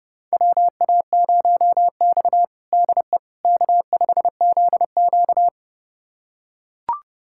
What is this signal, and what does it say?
Morse code, heard as a keyed tone, about 32 wpm, 700 Hz. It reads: WA0X DE K5ZQ
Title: USA Callsign DE Callsign - 4 Characters - 30wpm